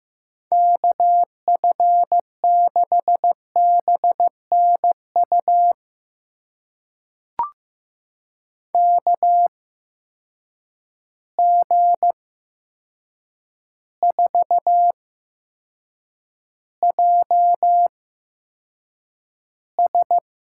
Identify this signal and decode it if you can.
Morse code, heard as a keyed tone, about 15 wpm, spaced as for 7 wpm, 700 Hz. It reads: KF6BNU KG4JS
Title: USA Call Signs - 6 Characters - 8x Character Spacing - 15wpm